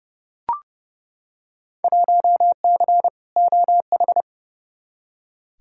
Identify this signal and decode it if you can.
Morse code, heard as a keyed tone, about 30 wpm, 700 Hz. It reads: 1C OH
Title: Field Day Exchanges - 30wpm